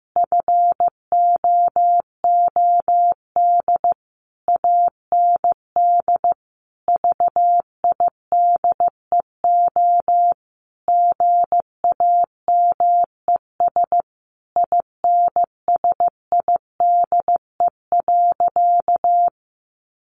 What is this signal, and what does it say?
Morse code, heard as a keyed tone, about 15 wpm, 700 Hz. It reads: FOOD AND VIDEO GAMES INSIDE.
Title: Easy Phrases 15wpm